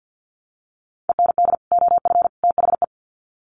Morse code, heard as a keyed tone, about 51 wpm, 700 Hz. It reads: END OF THE
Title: Sets of 3 Words 50wpm